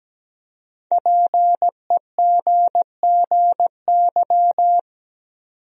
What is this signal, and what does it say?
Morse code, heard as a keyed tone, about 17 wpm, 700 Hz. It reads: PEGGY
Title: USA Common Names - Max 5 Letters - 17wpm